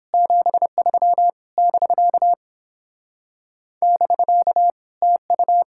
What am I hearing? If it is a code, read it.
Morse code, heard as a keyed tone, about 29 wpm, 700 Hz. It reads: 73 <BK> <BK> TU